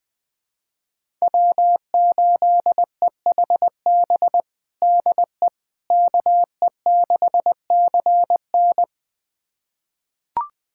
Morse code, heard as a keyed tone, about 20 wpm, 700 Hz. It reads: W8EHB DE KE6CN
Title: USA Callsign DE Callsign - 5 Characters - 20wpm